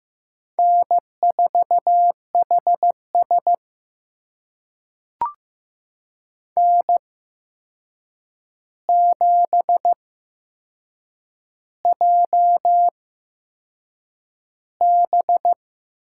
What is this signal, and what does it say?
Morse code, heard as a keyed tone, about 15 wpm, spaced as for 6 wpm, 700 Hz. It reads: N4HS N7JB
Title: USA Call Signs - 4 Characters - 8x Character Spacing - 15wpm